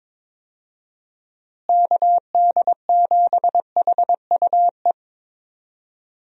Morse code, heard as a keyed tone, about 22 wpm, 700 Hz. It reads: KD7HUE
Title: USA Call Signs - 6 Characters - Repeated 3x - 22wpm